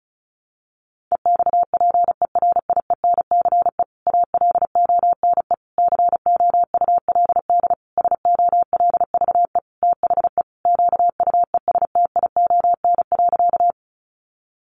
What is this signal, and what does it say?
Morse code, heard as a keyed tone, about 34 wpm, 700 Hz. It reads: EXPERIENCE ALONE COULD SOLVE THE QUESTION.
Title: Sentences from Top 1000 Words 35wpm